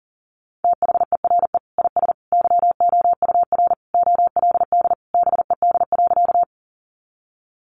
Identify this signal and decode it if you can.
Morse code, heard as a keyed tone, about 40 wpm, 700 Hz. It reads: THERE IS YOUR OLD BED.
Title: Sentences from Top 500 Words 40wpm